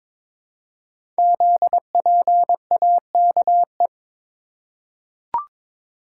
Morse code, heard as a keyed tone, about 22 wpm, 700 Hz. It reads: ZPAKE
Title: Code Groups - 5 Letter:Numbers - 22wpm